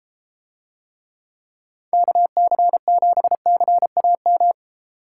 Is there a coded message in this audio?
Yes: Morse code from 1.9 to 4.5 s, heard as a keyed tone, about 33 wpm, 700 Hz. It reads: KC7CAM